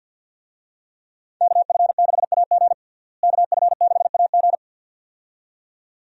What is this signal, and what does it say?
Morse code, heard as a keyed tone, about 52 wpm, 700 Hz. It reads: KF6AG KF6AG
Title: USA Call Signs - 5 Characters - Repeated 2x - 50wpm